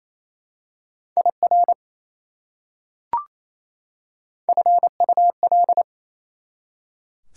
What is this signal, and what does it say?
Morse code, heard as a keyed tone, about 27 wpm, 700 Hz. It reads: IR FUL
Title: Prefixes and Suffixes 28wpm